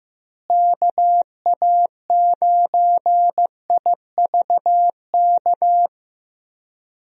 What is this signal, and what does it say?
Morse code, heard as a keyed tone, about 15 wpm, 700 Hz. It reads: KA9IVK
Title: USA Call Signs - 6 Characters - Repeated 2x - 15wpm